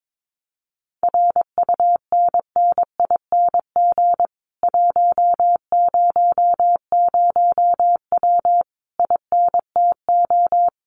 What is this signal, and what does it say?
Morse code, heard as a keyed tone, about 22 wpm, 700 Hz. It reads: RUNNING 100W INTO
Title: Common QSO Phrases Encore 22wpm